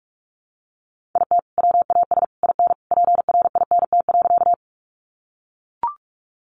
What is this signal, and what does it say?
Morse code, heard as a keyed tone, about 44 wpm, 700 Hz. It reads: IT WAS IN PRINT.
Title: Sentences from Top 1400 Words 45wpm